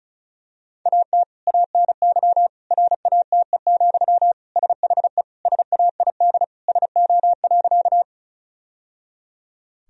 Morse code, heard as a keyed tone, about 34 wpm, 700 Hz. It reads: AT ANY RATE, SHE SAID SO.
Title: Sentences from Top 1000 Words - Review Part 4 - 35wpm